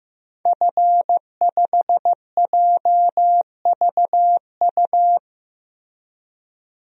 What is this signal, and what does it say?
Morse code, heard as a keyed tone, about 15 wpm, 700 Hz. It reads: F5JVU
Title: DXCC-100 Call Signs - 5 Characters - 15wpm